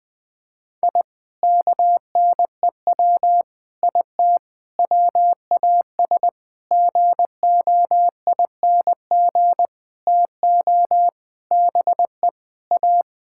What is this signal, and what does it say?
Morse code, heard as a keyed tone, about 20 wpm, 700 Hz. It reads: I KNEW IT WAS GOING TO BE A
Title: Sentences with 5-Word Phrases - Spoken Beginning - 20wpm